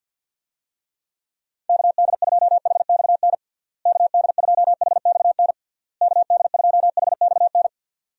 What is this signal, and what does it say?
Morse code, heard as a keyed tone, about 50 wpm, 700 Hz. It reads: KD2HXN KD2HXN KD2HXN
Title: USA Call Signs - 6 Characters - Repeated 3x - 50wpm